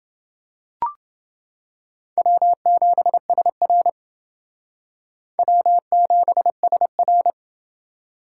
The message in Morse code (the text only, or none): W7SR W7SR